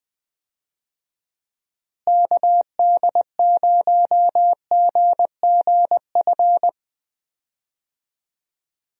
Morse code, heard as a keyed tone, about 20 wpm, 700 Hz. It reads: KD0GGF
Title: USA Call Signs - 6 Characters - Repeated 2x - 20wpm